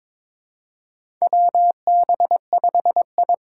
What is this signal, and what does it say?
Morse code, heard as a keyed tone, about 22 wpm, 700 Hz. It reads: W B 5 I
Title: USA Call Signs - 6 Characters - 8x Character Spacing - 22wpm